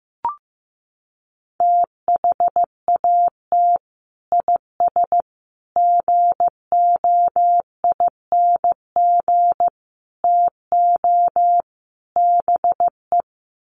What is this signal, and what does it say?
Morse code, heard as a keyed tone, about 15 wpm, 700 Hz. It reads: THAT IS GOING TO BE
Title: Sets of 5 Words Encore2 15wpm